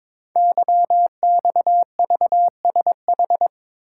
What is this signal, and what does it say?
Morse code, heard as a keyed tone, about 22 wpm, 700 Hz. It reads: YXVSH
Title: Code Groups - 5 Letter:Numbers - 22wpm